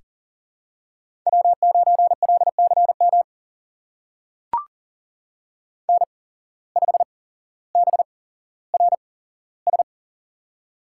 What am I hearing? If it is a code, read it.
Morse code, heard as a keyed tone, about 42 wpm, spaced as for 17 wpm, 700 Hz. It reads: W9LCM N5BRS
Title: USA Call Signs - 5 Characters - 8x Character Spacing - 40wpm